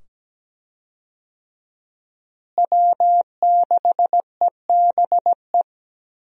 Morse code, heard as a keyed tone, about 17 wpm, 700 Hz. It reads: W6EBE
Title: USA Call Signs - 5 Characters - Repeated 2x - 17wpm